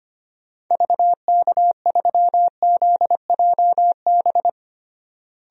Morse code, heard as a keyed tone, about 25 wpm, 700 Hz. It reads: VK3ZJB
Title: DXCC-100 Call Signs - 6 Characters - Repeated 2x - 25wpm